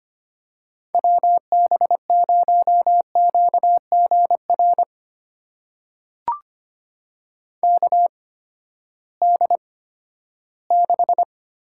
Morse code, heard as a keyed tone, about 25 wpm, spaced as for 14 wpm, 700 Hz. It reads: WB0QGR KD6
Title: USA Call Signs - 6 Characters - 8x Character Spacing - 25wpm